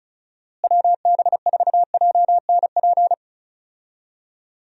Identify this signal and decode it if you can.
Morse code, heard as a keyed tone, about 34 wpm, 700 Hz. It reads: WB4JNP
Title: USA Call Signs - 6 Characters - Repeated 3x - 35wpm